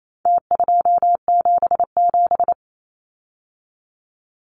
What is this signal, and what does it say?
Morse code, heard as a keyed tone, about 28 wpm, 700 Hz. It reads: T277
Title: Numbers - Contest Style - 4-Digit 28wpm